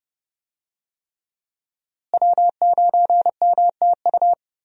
Morse code, heard as a keyed tone, about 30 wpm, 700 Hz. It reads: W9MTU
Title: USA Call Signs - 5 Characters - Repeated 3x - 30wpm